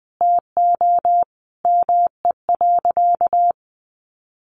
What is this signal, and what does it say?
Morse code, heard as a keyed tone, about 19 wpm, 700 Hz. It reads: TO ME.
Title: Sentences from Top 1200 Words 20wpm